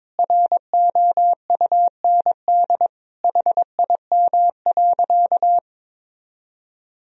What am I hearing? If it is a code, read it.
Morse code, heard as a keyed tone, about 22 wpm, 700 Hz. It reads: ROUND HIM.
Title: Sentences from Top 1000 Words - Review Part 1 - 22wpm